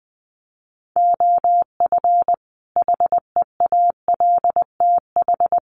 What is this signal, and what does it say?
Morse code, heard as a keyed tone, about 20 wpm, 700 Hz. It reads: OF HEALTH
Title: Sets of 5 Words 20wpm